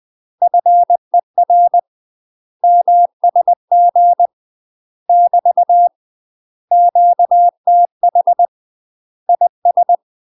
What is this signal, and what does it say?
Morse code, heard as a keyed tone, about 20 wpm, 700 Hz. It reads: FER MSG <BT> QTH IS
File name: UK MASTER QSOs TRAINING/20WPM